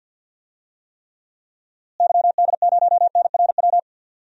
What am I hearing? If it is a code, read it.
Morse code, heard as a keyed tone, about 52 wpm, 700 Hz. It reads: YD0NRW
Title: DXCC-100 Call Signs - 6 Characters - Repeated 3x - 50wpm